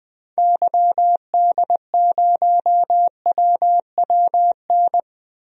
Morse code, heard as a keyed tone, about 20 wpm, 700 Hz. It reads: YD0WWN
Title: DXCC-100 Call Signs - 6 Characters - Repeated 3x - 20wpm